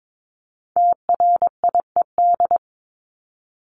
Morse code, heard as a keyed tone, about 22 wpm, 700 Hz. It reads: TRIED
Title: ED Suffix - Top 100 Words with Base Words 22wpm